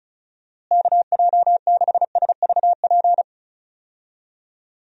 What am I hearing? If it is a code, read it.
Morse code, heard as a keyed tone, about 36 wpm, 700 Hz. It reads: KJ6SVP
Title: USA Call Signs - 6 Characters - Repeated 2x - 35wpm